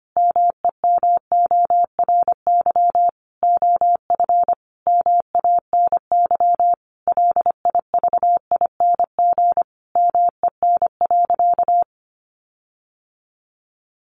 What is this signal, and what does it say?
Morse code, heard as a keyed tone, about 25 wpm, 700 Hz. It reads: MEMORY OF MANY LIVING MEN.
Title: Sentences from Top 1300 Words 25wpm